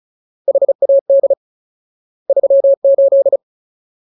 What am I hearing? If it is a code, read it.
Morse code, heard as a keyed tone, about 35 wpm, 550 Hz. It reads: HAD 38
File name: Ultimate Word List / 35WPM Ultimate